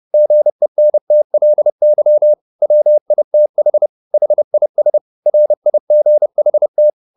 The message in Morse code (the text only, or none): GENTLY WITH HIS RIGHT